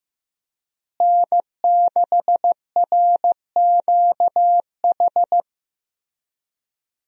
Morse code, heard as a keyed tone, about 16 wpm, 700 Hz. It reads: N6RQH
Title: USA Call Signs - 5 Characters - Repeated 3x - 15wpm